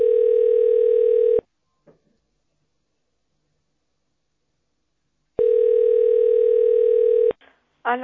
Ringback tone (North America), 440 + 480 Hz.